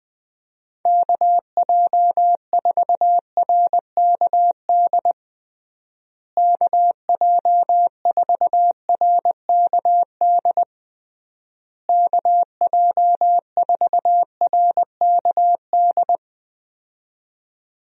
Morse code, heard as a keyed tone, about 21 wpm, 700 Hz. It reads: KJ4RKD KJ4RKD KJ4RKD
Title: USA Call Signs - 6 Characters - Repeated 3x - 20wpm